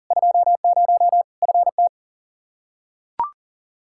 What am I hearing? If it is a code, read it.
Morse code, heard as a keyed tone, about 40 wpm, 700 Hz. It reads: 20 FT
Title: Practice QSO - Exchange 2 - Info - 40wpm